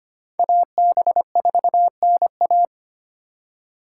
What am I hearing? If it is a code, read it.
Morse code, heard as a keyed tone, about 32 wpm, 700 Hz. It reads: AB4NA